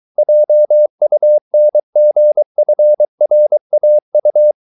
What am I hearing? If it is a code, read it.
Morse code, heard as a keyed tone, about 23 wpm, 600 Hz. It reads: JUNGFRAU